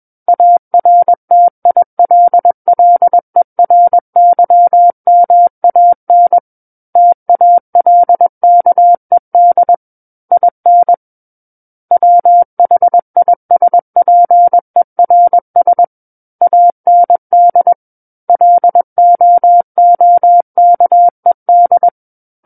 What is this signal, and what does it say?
Morse code, heard as a keyed tone, about 21 wpm, 700 Hz. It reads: ARTILLERYMAN TALKED IN WHISPERS AND LOOKED
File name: Book One / 12-Chapter 12 - 21 WPM